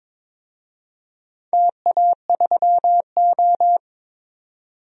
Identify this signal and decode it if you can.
Morse code, heard as a keyed tone, about 22 wpm, 700 Hz. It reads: TA3O
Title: DXCC-100 Call Signs - 4 Characters - 22wpm